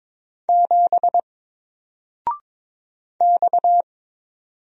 Morse code, heard as a keyed tone, about 22 wpm, 700 Hz. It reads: 7 X
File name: Single-Letter-Number-22wpm